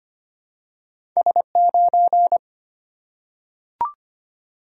Morse code, heard as a keyed tone, about 25 wpm, 700 Hz. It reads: S9
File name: Two Letter-Number Combinations - 25wpm